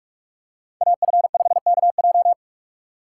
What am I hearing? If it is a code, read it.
Morse code, heard as a keyed tone, about 46 wpm, 700 Hz. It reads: AF5KJ
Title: USA Call Signs - 5 Characters - 2x Character Spacing - 45wpm